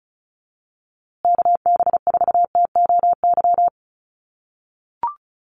Morse code, heard as a keyed tone, about 34 wpm, 700 Hz. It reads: KB4TOY
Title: USA Call Signs Repeated 2x 35wpm